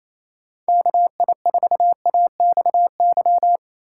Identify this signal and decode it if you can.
Morse code, heard as a keyed tone, about 28 wpm, 700 Hz. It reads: KI4AXY
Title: USA Call Signs - 6 Characters - 4x Character Spacing - 28wpm